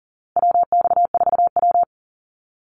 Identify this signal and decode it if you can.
Morse code, heard as a keyed tone, about 40 wpm, 700 Hz. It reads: WX4W
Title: Call Signs Repeated 2x - 40wpm